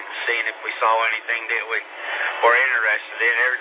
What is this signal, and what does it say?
Emergency Alert System (SAME) tones, heard as harsh buzzing bursts, with speech. It is heard from 0.2 to 3.2 s.